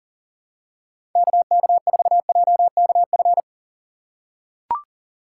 Morse code, heard as a keyed tone, about 40 wpm, 700 Hz. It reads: KK4JKF
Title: USA Call Signs - 6 Characters - 2x Character Spacing - 40wpm